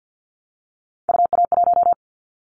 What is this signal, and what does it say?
Morse code, heard as a keyed tone, about 50 wpm, 700 Hz. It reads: UA1
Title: Callsign Prefixes for Top 50 DXCC - 50wpm